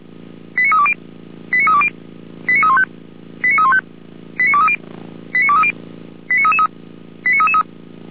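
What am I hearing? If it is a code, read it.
Five-tone selective call (ZVEI), heard as quick runs of five tones, tones of about 70 ms. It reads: 89320 89320 89315 89315 89230 89230 89333 89333
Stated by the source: Five-tone ZVEI selective call received by SDR, from the Orlamünde fire brigade (feuerwehr-orlamuende) server.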